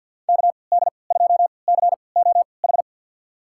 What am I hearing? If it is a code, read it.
Morse code, heard as a keyed tone, about 48 wpm, spaced as for 28 wpm, 700 Hz. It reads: KD2COH
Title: USA Call Signs - 3x Character Spacing - 50wpm